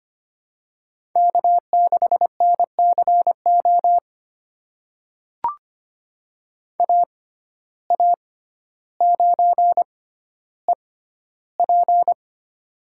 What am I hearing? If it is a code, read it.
Morse code, heard as a keyed tone, about 25 wpm, spaced as for 13 wpm, 700 Hz. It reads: K6NCO AA9EP
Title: USA Call Signs - 5 Characters - 6x Character Spacing - 25wpm